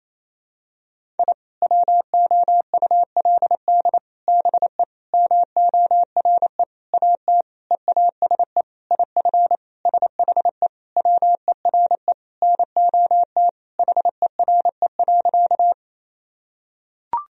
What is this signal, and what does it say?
Morse code, heard as a keyed tone, about 28 wpm, 700 Hz. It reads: I WOULD BE MORE AT EASE IF SHE WERE NOT HERE.
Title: Sentences from Top 1000 Words - Review Part 1 - 28wpm